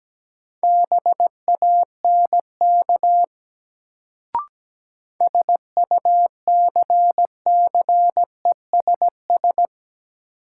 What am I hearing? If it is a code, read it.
Morse code, heard as a keyed tone, about 17 wpm, 700 Hz. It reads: BANK SUCCESS
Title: Sentences from Top 900 Words - New Word Drill - 17wpm